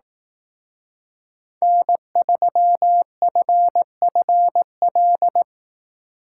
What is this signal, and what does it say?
Morse code, heard as a keyed tone, about 19 wpm, 700 Hz. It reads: N3FFL
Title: Speed Racing - USA 1x3 Call Signs 15wpm